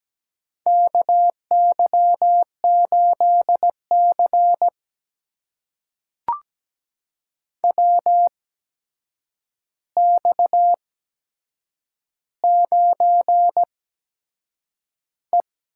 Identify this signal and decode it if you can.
Morse code, heard as a keyed tone, about 17 wpm, spaced as for 7 wpm, 700 Hz. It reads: KY8C WX9E
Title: USA Call Signs - 4 Characters - 8x Character Spacing - 17wpm